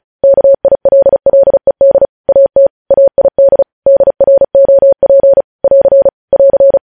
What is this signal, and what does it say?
Morse code, heard as a keyed tone, about 36 wpm, 550 Hz. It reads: KILLED AT AID DROP <AR> <AR>
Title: News Headlines at 35